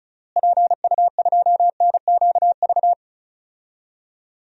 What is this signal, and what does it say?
Morse code, heard as a keyed tone, about 34 wpm, 700 Hz. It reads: PU2NQV
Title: DXCC-100 Call Signs - 6 Characters - Repeated 3x - 35wpm